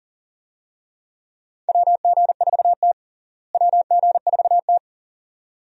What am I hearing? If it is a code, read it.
Morse code, heard as a keyed tone, about 37 wpm, 700 Hz. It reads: WG4T WG4T
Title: USA Call Signs - 4 Characters - Repeated 2x - 40wpm